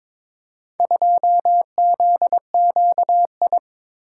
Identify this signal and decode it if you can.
Morse code, heard as a keyed tone, about 21 wpm, 700 Hz. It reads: 2ZQI
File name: practice / 22wpm / Four Letter-Number Combinations - 22wpm